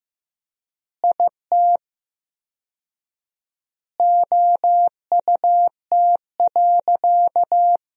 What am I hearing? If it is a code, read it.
Morse code, heard as a keyed tone, about 16 wpm, 700 Hz. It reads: IT OUT.
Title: 4x Word-Spacing - Sentences from Top 300 Words - 15wpm